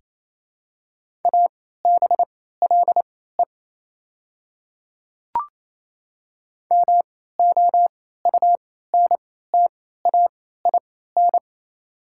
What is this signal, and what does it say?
Morse code, heard as a keyed tone, about 28 wpm, spaced as for 16 wpm, 690 Hz. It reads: ABLE MOUNTAIN
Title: 3x Character Spacing - Top 500 Words - 28wpm